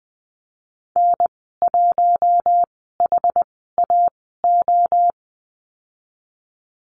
Morse code, heard as a keyed tone, about 20 wpm, spaced as for 15 wpm, 700 Hz. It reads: N1HAO
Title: 062 Lesson 11g - Letter h - Call Signs 20wpm